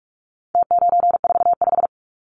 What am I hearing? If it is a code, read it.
Morse code, heard as a keyed tone, about 44 wpm, 700 Hz. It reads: T945
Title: Numbers - Contest Style - 4-Digit 45wpm